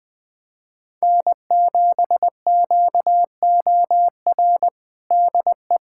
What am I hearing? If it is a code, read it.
Morse code, heard as a keyed tone, about 20 wpm, 700 Hz. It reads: N7QOR DE
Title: USA Callsign DE Callsign 20wpm